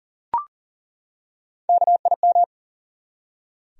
Morse code, heard as a keyed tone, about 40 wpm, 700 Hz. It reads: KIM
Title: USA Common Names - Max 4 Letters - 40wpm